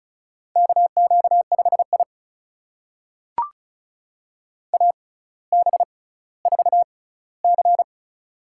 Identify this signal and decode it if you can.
Morse code, heard as a keyed tone, about 36 wpm, spaced as for 18 wpm, 700 Hz. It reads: KQ5I AB4C
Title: USA Call Signs - 4 Characters - 6x Character Spacing - 35wpm